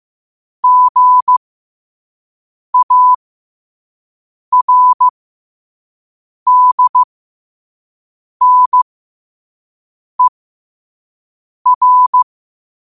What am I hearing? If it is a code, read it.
Morse code, heard as a keyed tone, about 15 wpm, spaced as for 5 wpm, 1000 Hz. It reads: GARDNER
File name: qotdfiles / 5wpm